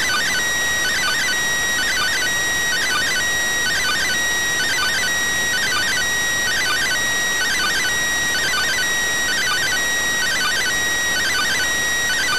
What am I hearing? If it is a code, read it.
Two-tone teleprinter signal, about 400 Hz shift.